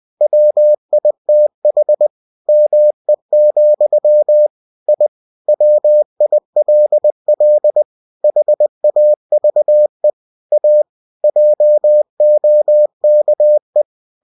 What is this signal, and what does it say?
Morse code, heard as a keyed tone, about 20 wpm, 600 Hz. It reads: WITH ME, I WILL HAVE A JOKE
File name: cw-books / winnie-the-pooh / winnie-20wpm-page-0071